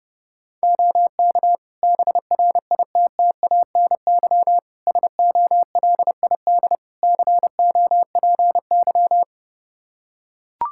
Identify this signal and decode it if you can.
Morse code, heard as a keyed tone, about 30 wpm, 700 Hz. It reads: OK BRITTANY SOLID COPY
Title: K1USN SST QSO Practice 30wpm